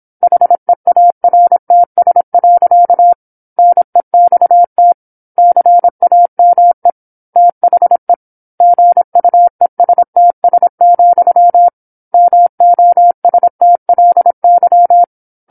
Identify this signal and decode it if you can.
Morse code, heard as a keyed tone, about 27 wpm, 700 Hz. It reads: HEARTS. NEXT CAME THE GUESTS, MOSTLY
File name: 08-Chapter-08 26 WPM